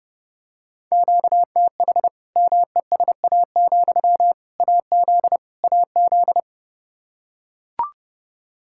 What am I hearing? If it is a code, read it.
Morse code, heard as a keyed tone, about 32 wpm, 700 Hz. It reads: QTH MESA, AZ AZ
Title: Practice QSO - Exchange 1 - Beginning - 30wpm